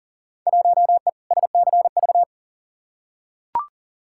Morse code, heard as a keyed tone, about 40 wpm, 700 Hz. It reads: 1E SCV
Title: Field Day Exchanges - 40wpm